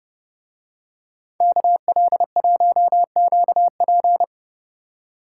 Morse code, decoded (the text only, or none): KL1QP